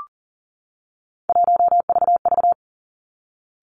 Morse code, heard as a keyed tone, about 40 wpm, 700 Hz. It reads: 1VV
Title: Three Challenging Letter-Number Combinations 40wpm